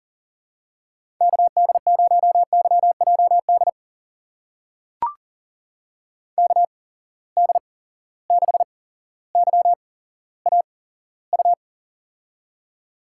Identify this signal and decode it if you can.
Morse code, heard as a keyed tone, about 41 wpm, spaced as for 17 wpm, 700 Hz. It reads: KD0YJD KD6YAU
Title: USA Call Signs - 6 Characters - 8x Character Spacing - 40wpm